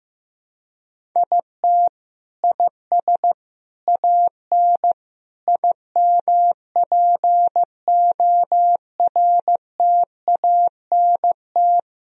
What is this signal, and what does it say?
Morse code, heard as a keyed tone, about 16 wpm, 700 Hz. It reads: IT IS AN IMPORTANT
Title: Sentences from Top 1000 Words - Review Part 1 - 15wpm